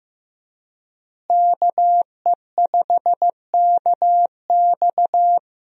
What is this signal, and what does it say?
Morse code, heard as a keyed tone, about 15 wpm, 700 Hz. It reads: KE5KX
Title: USA Call Signs - 5 Characters - Repeated 3x - 15wpm